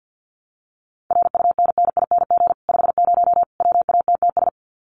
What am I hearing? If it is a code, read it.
Morse code, heard as a keyed tone, about 50 wpm, 700 Hz. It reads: RUNNING 50 WATTS
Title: Common QSO Phrases 50wpm